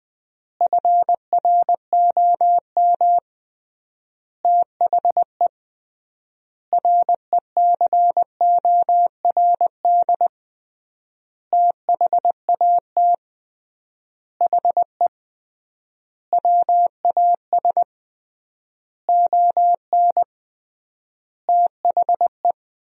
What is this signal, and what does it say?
Morse code, heard as a keyed tone, about 20 wpm, 700 Hz. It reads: FROM THE RECORD THAT HE WAS ON THE
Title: 3x Word-Spacing - Sentences from Top 500 Words - 20wpm